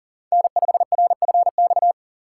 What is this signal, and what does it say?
Morse code, heard as a keyed tone, about 40 wpm, 700 Hz. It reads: N5RFX
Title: USA Call Signs - 5 Characters - 2x Character Spacing - 40wpm